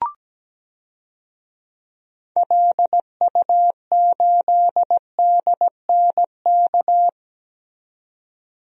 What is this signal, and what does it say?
Morse code, heard as a keyed tone, about 17 wpm, 700 Hz. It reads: LU8DNK
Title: DXCC-100 Call Signs - 6 Characters - 17wpm